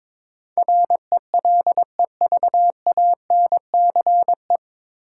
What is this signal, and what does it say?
Morse code, heard as a keyed tone, about 22 wpm, 700 Hz. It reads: RELEVANCE
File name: 9 Letter Words - First 3 Letters Given - 22wpm